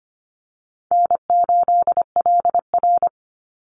Morse code, heard as a keyed tone, about 25 wpm, 700 Hz. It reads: N8LR
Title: Easy Call Signs - 25wpm